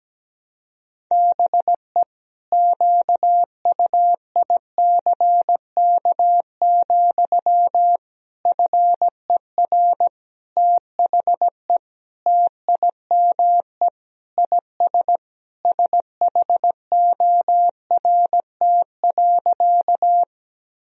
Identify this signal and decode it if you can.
Morse code, heard as a keyed tone, about 17 wpm, 700 Hz. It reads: BE QUICK, FER THE TIME IS SHORT.
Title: Sentences from Top 500 Words and Abbreviations 17wpm